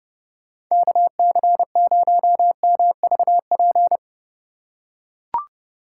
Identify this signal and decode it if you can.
Morse code, heard as a keyed tone, about 30 wpm, 700 Hz. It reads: KC0MVP